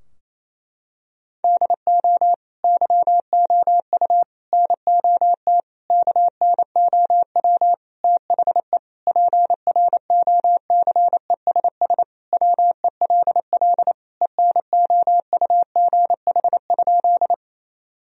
Morse code, heard as a keyed tone, about 27 wpm, 700 Hz. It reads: DO YOU NOT KNOW THE PROCESS WELL ENOUGH?
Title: Sentences from Top 1000 Words - Review Part 3 - 28wpm